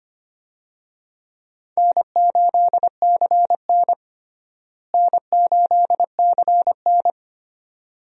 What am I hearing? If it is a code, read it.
Morse code, heard as a keyed tone, about 25 wpm, 700 Hz. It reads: N8CN N8CN
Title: USA Call Signs - 4 Characters - Repeated 2x - 25wpm